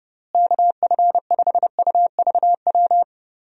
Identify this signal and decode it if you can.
Morse code, heard as a keyed tone, about 30 wpm, 700 Hz. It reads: KF5UVW
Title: USA Call Signs - 6 Characters - Repeated 2x - 30wpm